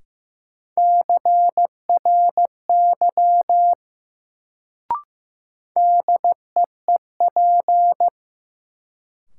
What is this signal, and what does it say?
Morse code, heard as a keyed tone, about 15 wpm, 700 Hz. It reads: CRY DEEP
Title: Sets of 5 Words Encore - New Word Drill - 15wpm